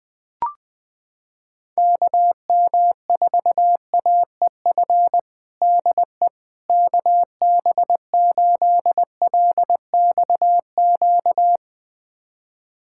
Morse code, heard as a keyed tone, about 20 wpm, 700 Hz. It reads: KM4AEF DE KB8LXQ
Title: USA Callsign DE Callsign - 6 Characters - 20wpm